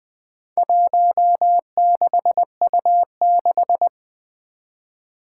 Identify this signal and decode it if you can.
Morse code, heard as a keyed tone, about 20 wpm, 700 Hz. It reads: 16U6
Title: Four Challenging Letter-Number Combinations 20wpm